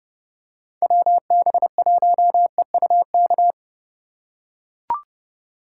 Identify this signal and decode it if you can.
Morse code, heard as a keyed tone, about 30 wpm, 700 Hz. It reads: WB1EUK